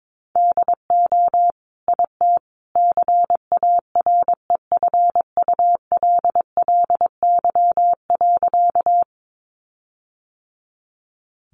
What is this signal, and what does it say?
Morse code, heard as a keyed tone, about 22 wpm, 700 Hz. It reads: DO IT CAREFULLY.
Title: Sentences from Top 1300 Words 22wpm